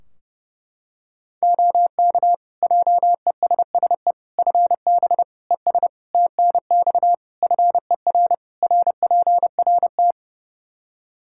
Morse code, heard as a keyed tone, about 30 wpm, 700 Hz. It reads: OK JESSE FB ES TNX FER RPRT
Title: Practice QSO - Exchange 2 - Info - 30wpm